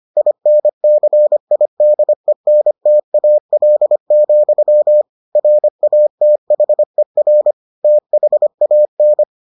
Morse code, heard as a keyed tone, about 25 wpm, 600 Hz. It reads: INCIDENTAL, RATHER THAN